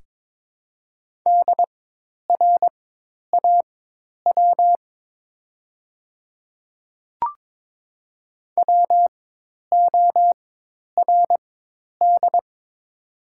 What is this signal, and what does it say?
Morse code, heard as a keyed tone, about 22 wpm, spaced as for 10 wpm, 700 Hz. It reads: DRAW WORD
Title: 4x Character Spacing - Top 500 Words - 22wpm